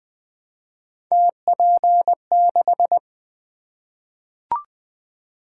Morse code, heard as a keyed tone, about 20 wpm, 700 Hz. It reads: TP6
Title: Three Letter-Number Combinations - 20wpm